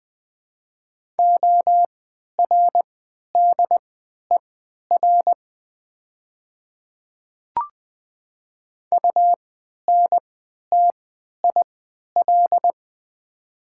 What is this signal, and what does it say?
Morse code, heard as a keyed tone, about 20 wpm, spaced as for 11 wpm, 700 Hz. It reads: ORDER UNTIL
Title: 3x Character Spacing - Top 500 Words - 20wpm